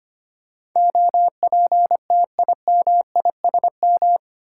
Morse code, heard as a keyed tone, about 25 wpm, 700 Hz. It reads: OPTIMISM
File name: Eight Letter Words - 25wpm